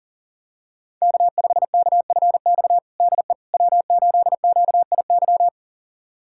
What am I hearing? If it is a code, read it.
Morse code, heard as a keyed tone, about 40 wpm, 700 Hz. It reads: K5KFX DE W8QIY
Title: USA Callsign DE Callsign - 5 Characters - 40wpm